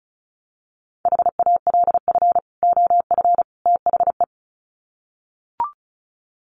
Morse code, heard as a keyed tone, about 34 wpm, 700 Hz. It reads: HALF OF THE
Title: Sets of 3 Words 35wpm